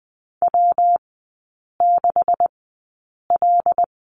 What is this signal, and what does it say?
Morse code, heard as a keyed tone, about 20 wpm, spaced as for 8 wpm, 690 Hz. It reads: W6L
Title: 145 Lesson 24e - Letter y - Three Characters 20wpm